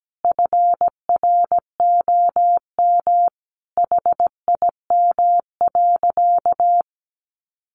Morse code, heard as a keyed tone, about 17 wpm, 700 Hz. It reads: FROM HIM.
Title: Sentences from Top 400 Words 17wpm